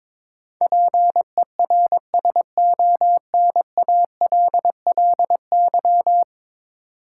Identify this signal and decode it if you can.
Morse code, heard as a keyed tone, about 22 wpm, 700 Hz. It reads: PERSONALLY
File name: practice / 22wpm / 10 Letter Words - First 3 Letters Given - 22wpm